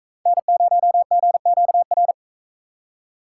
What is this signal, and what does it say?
Morse code, heard as a keyed tone, about 44 wpm, 700 Hz. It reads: N0GQR